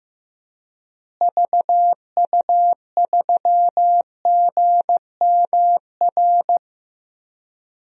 Morse code, heard as a keyed tone, about 15 wpm, 700 Hz. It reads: VU3GMR
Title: DXCC-100 Call Signs - 6 Characters - Repeated 2x - 15wpm